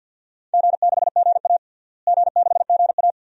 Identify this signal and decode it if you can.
Morse code, heard as a keyed tone, about 50 wpm, 700 Hz. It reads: G6GA G6GA
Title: DXCC-100 Call Signs - 4 Characters - Repeated 2x - 50wpm